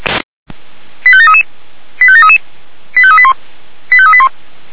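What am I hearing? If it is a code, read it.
Five-tone selective call (ZVEI), heard as quick runs of five tones, tones of about 70 ms. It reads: 85733 85733 85381 85381
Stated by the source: ZVEI five-tone selective call received on an SDR.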